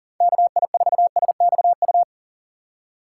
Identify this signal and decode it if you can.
Morse code, heard as a keyed tone, about 39 wpm, 700 Hz. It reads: KI4SXU
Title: USA Call Signs - 6 Characters - 4x Character Spacing - 40wpm